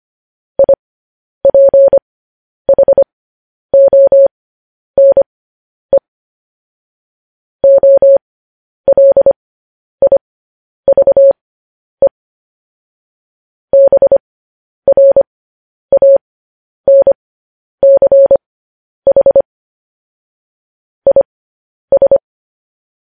Morse code, heard as a keyed tone, about 25 wpm, spaced as for 10 wpm, 550 Hz. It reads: IPHONE OLIVE BRANCH IS
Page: News Headlines in Morse Code at 10 WPM